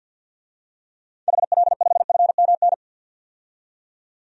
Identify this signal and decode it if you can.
Morse code, heard as a keyed tone, about 52 wpm, 700 Hz. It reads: HL5FMN
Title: DXCC-100 Call Signs - 6 Characters - 50wpm